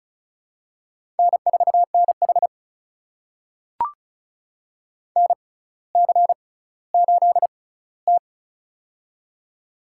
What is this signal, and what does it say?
Morse code, heard as a keyed tone, about 36 wpm, spaced as for 18 wpm, 700 Hz. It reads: N4NH NC8T
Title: USA Call Signs - 4 Characters - 6x Character Spacing - 35wpm